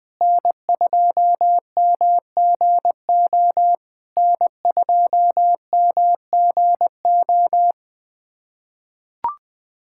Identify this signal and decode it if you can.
Morse code, heard as a keyed tone, about 21 wpm, 700 Hz. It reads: N2MGO N2MGO
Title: SOTA QSO Practice 20wpm